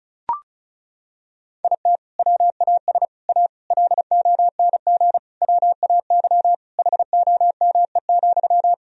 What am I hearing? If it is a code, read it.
Morse code, heard as a keyed tone, about 34 wpm, 700 Hz. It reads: IT WAS A LONG WAY HOME,
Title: Sentences from Top 1000 Words - Review Part 2 - 35wpm